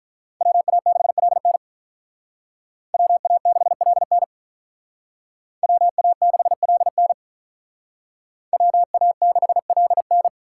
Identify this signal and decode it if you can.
Morse code, heard as a keyed tone, about 43 wpm, 700 Hz. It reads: WA6LN WA6LN WA6LN WA6LN
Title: Speed Racing - USA Call Signs 35wpm